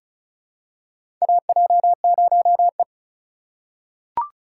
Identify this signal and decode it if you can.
Morse code, heard as a keyed tone, about 36 wpm, 700 Hz. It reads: AJ0E
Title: USA Call Signs - 4 Characters - 4x Character Spacing - 35wpm